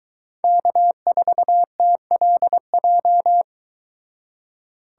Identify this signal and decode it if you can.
Morse code, heard as a keyed tone, about 23 wpm, 700 Hz. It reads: K4TLJ